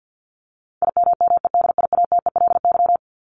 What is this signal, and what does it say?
Morse code, heard as a keyed tone, about 50 wpm, 700 Hz. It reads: IMMEDIATELY
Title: LY Suffix - Top 100 Words with Base Words 50wpm